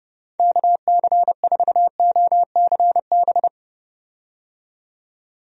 Morse code, heard as a keyed tone, about 32 wpm, 700 Hz. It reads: KC4OCB
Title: USA Call Signs - 6 Characters - 30wpm